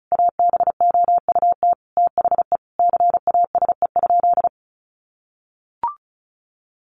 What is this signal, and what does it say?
Morse code, heard as a keyed tone, about 34 wpm, 700 Hz. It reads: ABOUT THE CASE?
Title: Sentences from Top 1000 Words 35wpm